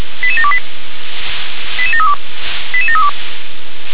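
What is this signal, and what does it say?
Five-tone selective call (ZVEI), heard as quick runs of five tones, tones of about 70 ms. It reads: 88728 88732 88732
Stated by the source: Received on an SDR; ZVEI five-tone selective call.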